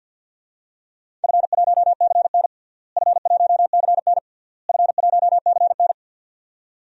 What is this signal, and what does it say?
Morse code, heard as a keyed tone, about 52 wpm, 700 Hz. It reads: F1CN F1CN F1CN
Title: DXCC-100 Call Signs - 4 Characters - Repeated 3x - 50wpm